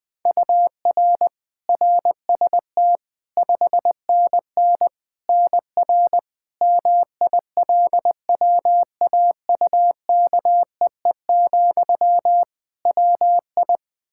Morse code, heard as a keyed tone, about 20 wpm, 700 Hz. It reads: UR RST 5NN NR MILWAUKEE, WI
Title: Practice QSO - Exchange 1 - Beginning - 20wpm